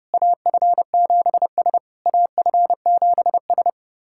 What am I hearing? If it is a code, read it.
Morse code, heard as a keyed tone, about 30 wpm, 700 Hz. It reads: AF7S AF7S
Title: SOTA QSO Practice 30wpm